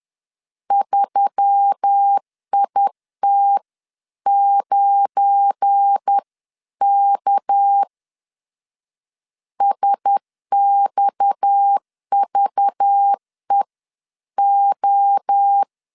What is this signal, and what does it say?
Morse code, heard as a keyed tone, about 10 wpm, spaced as for 9 wpm, 800 Hz. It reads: 3IT9K SXVEO
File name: LEZIONI / 7 GF9Z / 7 GF9 10wpm 2747